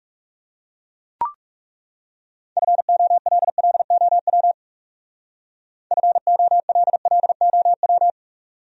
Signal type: Morse code

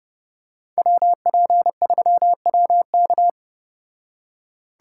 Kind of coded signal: Morse code

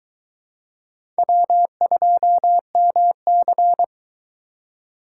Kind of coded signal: Morse code

